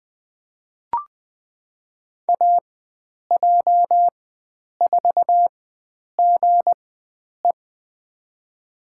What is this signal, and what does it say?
Morse code, heard as a keyed tone, about 20 wpm, spaced as for 9 wpm, 700 Hz. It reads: AJ4GE
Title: USA Call Signs - 4x Character Spacing - 20wpm